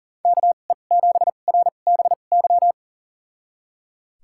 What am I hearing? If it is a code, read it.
Morse code, heard as a keyed tone, about 39 wpm, spaced as for 29 wpm, 700 Hz. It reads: KE7RBY
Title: USA Call Signs - 6 Characters - 2x Character Spacing - 40wpm